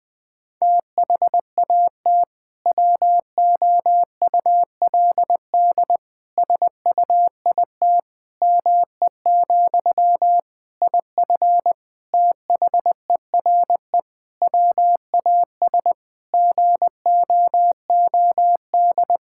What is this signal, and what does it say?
Morse code, heard as a keyed tone, about 20 wpm, 700 Hz. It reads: THAT WOULD SUIT ME, IF THERE WAS GOOD
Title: Sentences from Top 1600 Words 20wpm